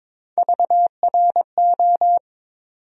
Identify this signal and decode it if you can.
Morse code, heard as a keyed tone, about 22 wpm, 700 Hz. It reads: VRO
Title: USA Call Sign Postfixes - 3 Letters 22wpm